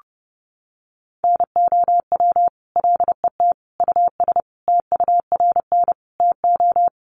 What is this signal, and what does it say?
Morse code, heard as a keyed tone, about 30 wpm, 700 Hz. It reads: NOW LET US TURN TO
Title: Sentences from Top 600 Words 30wpm